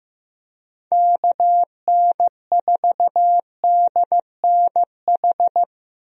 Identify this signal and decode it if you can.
Morse code, heard as a keyed tone, about 15 wpm, 700 Hz. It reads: KN4DNH
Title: USA Call Signs - 6 Characters - Repeated 3x - 15wpm